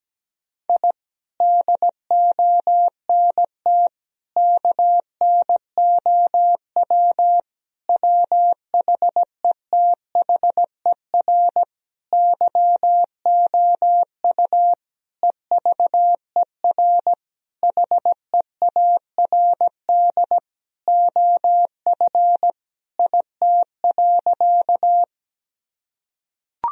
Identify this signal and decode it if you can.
Morse code, heard as a keyed tone, about 17 wpm, 700 Hz. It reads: I DONT KNOW WHETHER YOU EVER HEARD OF IT.
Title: Sentences from Top 1000 Words - Review Part 3 - 17wpm